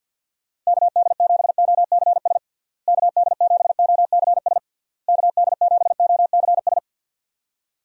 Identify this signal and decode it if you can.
Morse code, heard as a keyed tone, about 48 wpm, 700 Hz. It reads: KD7OCS KD7OCS KD7OCS
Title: USA Call Signs - 6 Characters - Repeated 3x - 50wpm